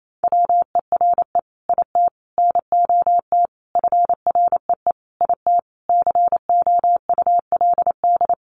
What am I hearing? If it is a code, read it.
Morse code, heard as a keyed tone, about 28 wpm, 700 Hz. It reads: WERE IT NOT FREE IT COULD
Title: Sentences from Top 900 Words 28wpm